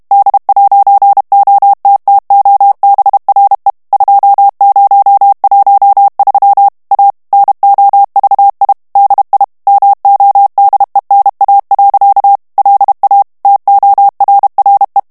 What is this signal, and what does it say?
Morse code, heard as a keyed tone, about 32 wpm, 800 Hz. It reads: D'OTTOBRE 2013 A NOVI DI MODENA. LA TORRE
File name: morse mp3 / ITALIEN texte clair 1 vitesse 30 mots minute